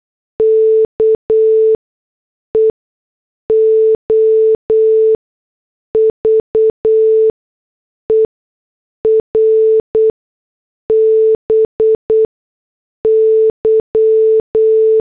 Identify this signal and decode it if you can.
Morse code, heard as a keyed tone, about 8 wpm, 440 Hz. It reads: K E O V E R B Y
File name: morse / headlines